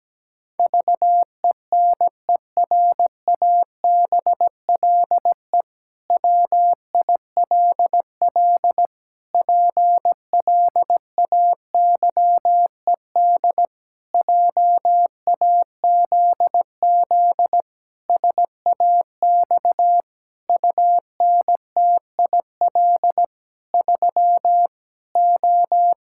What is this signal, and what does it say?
Morse code, heard as a keyed tone, about 17 wpm, 700 Hz. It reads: VENERABLE WILL PLAYED JAZZ SAX UNTIL 3 O
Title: Pangram Challenge v2 - 17wpm